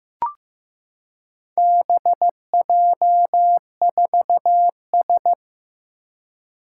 Morse code, heard as a keyed tone, about 15 wpm, 700 Hz. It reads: BJ4S
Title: Four Challenging Letter-Number Combinations 15wpm